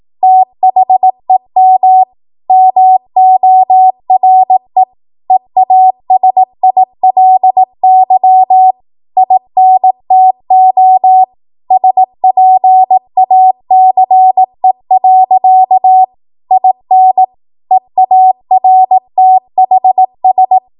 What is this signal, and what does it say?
Morse code, heard as a keyed tone, about 18 wpm, 750 Hz. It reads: THEM MORE EASILY INTO SPACE. IN EARTHS